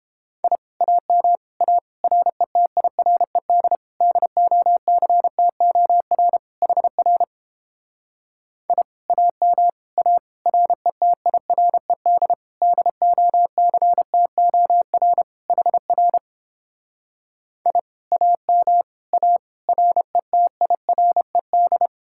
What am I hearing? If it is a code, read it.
Morse code, heard as a keyed tone, about 31 wpm, 700 Hz. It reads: I AM A RETIRED DOCTOR HR I AM A RETIRED DOCTOR HR I AM A RETIRED